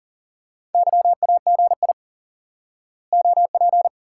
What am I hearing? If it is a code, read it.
Morse code, heard as a keyed tone, about 39 wpm, 700 Hz. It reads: YAGI OP
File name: IQE Course 3 - Lesson 89 - YAGI - Two Elements - 40wpm